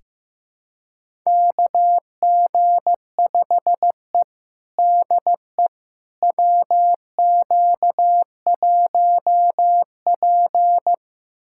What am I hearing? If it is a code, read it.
Morse code, heard as a keyed tone, about 15 wpm, 700 Hz. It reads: KG5E DE WQ1P
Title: USA Callsign DE Callsign - 4 Characters - 15wpm